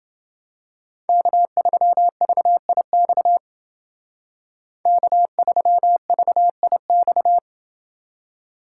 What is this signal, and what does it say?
Morse code, heard as a keyed tone, about 28 wpm, 700 Hz. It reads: K3VIX K3VIX